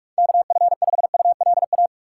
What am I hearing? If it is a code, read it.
Morse code, heard as a keyed tone, about 44 wpm, 700 Hz. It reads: KF5ULA
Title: USA Call Signs - 6x Character Spacing - 45wpm